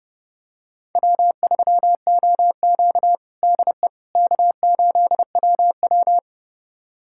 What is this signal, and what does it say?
Morse code, heard as a keyed tone, about 30 wpm, 700 Hz. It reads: W3OQ DE K8WW
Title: USA Callsign DE Callsign - 4 Characters - 30wpm